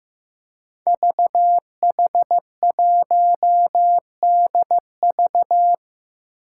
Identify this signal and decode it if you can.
Morse code, heard as a keyed tone, about 15 wpm, 700 Hz. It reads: VH1DV
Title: Five Challenging Letter-Number Combinations 15wpm